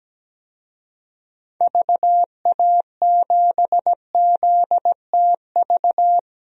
Morse code, heard as a keyed tone, about 17 wpm, 700 Hz. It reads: VA7ZTV
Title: DXCC-100 Call Signs - 6 Characters - Repeated 2x - 17wpm